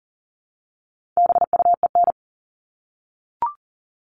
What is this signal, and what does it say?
Morse code, heard as a keyed tone, about 40 wpm, 700 Hz. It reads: BUEN
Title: All Cut Numbers - 4-Digit 40wpm